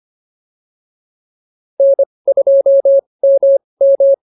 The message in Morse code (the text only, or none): N2MM